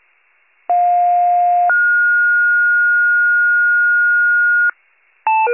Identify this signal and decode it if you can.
Two-tone page, heard as a one-second tone, then a three-second tone, about 710 then 1430 Hz. It is heard from 0.7 to 4.7 s.